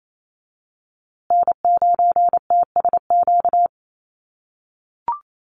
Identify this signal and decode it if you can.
Morse code, heard as a keyed tone, about 28 wpm, 700 Hz. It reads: N9TSQ
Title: USA Call Signs Repeated 2x 28wpm